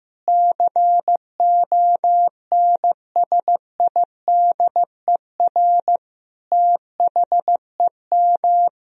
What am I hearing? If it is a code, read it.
Morse code, heard as a keyed tone, about 15 wpm, 700 Hz. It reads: CONSIDER THEM
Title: Sentences from Top 1000 Words - Review Part 4 - 15wpm